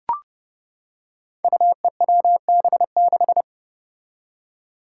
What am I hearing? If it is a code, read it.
Morse code, heard as a keyed tone, about 28 wpm, 700 Hz. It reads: UEWB6
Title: Five Challenging Letter-Number Combinations 30wpm